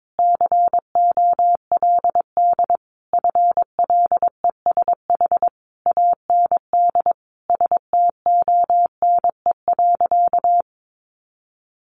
Morse code, heard as a keyed tone, about 22 wpm, 700 Hz. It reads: COLD FLESH AND STONE.
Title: A Sound Of Thunder 22wpm